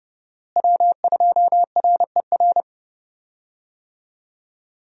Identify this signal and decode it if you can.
Morse code, heard as a keyed tone, about 30 wpm, 700 Hz. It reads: W2RER
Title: USA Call Signs - 5 Characters - Repeated 2x - 30wpm